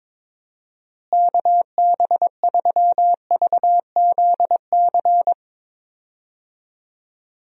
Morse code, heard as a keyed tone, about 22 wpm, 700 Hz. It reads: KB3VZC
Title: USA Call Signs - 6 Characters - Repeated 3x - 22wpm